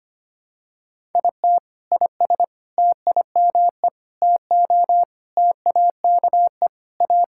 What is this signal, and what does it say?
Morse code, heard as a keyed tone, about 25 wpm, 700 Hz. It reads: IT IS TIME TO TAKE A
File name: practice / 25wpm / Sentences with 4-Word Phrases - 25wpm